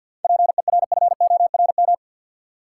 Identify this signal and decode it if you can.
Morse code, heard as a keyed tone, about 52 wpm, 700 Hz. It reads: PERFORM
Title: Sentences from Top 1000 Words - Review Part 5 - New Word Drill - 50wpm